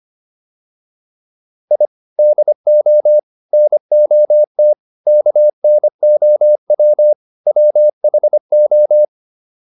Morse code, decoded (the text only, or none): I DO NOT KNOW WHO